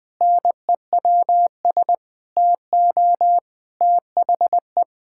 Morse code, heard as a keyed tone, about 20 wpm, 700 Hz. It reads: NEWS TO THE
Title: Pangram Challenge v2 - 20wpm